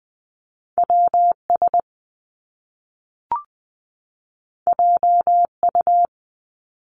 Morse code, heard as a keyed tone, about 20 wpm, 700 Hz. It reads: W S J U
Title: Top Two Letter Combinations in Words 20wpm